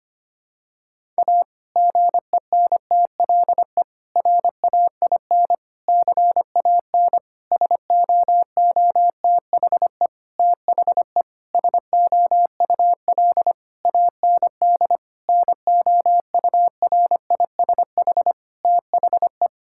Morse code, heard as a keyed tone, about 25 wpm, 700 Hz. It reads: A GENTLE RAIN CAN SOOTHE THE SOUL AND NOURISH THE
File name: practice / 25wpm / Difficult Sentences with Hints - 25wpm